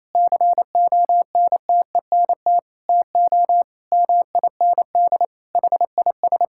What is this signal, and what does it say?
Morse code, heard as a keyed tone, about 28 wpm, 700 Hz. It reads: CONTENT TO MIND HIS
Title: Sentences from Top 1000 Words - Review Part 6 - 28wpm